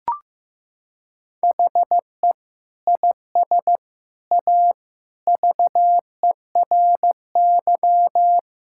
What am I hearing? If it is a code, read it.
Morse code, heard as a keyed tone, about 15 wpm, 700 Hz. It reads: HE IS A VERY
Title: Sentences from Top 1000 Words - Review Part 5 - 15wpm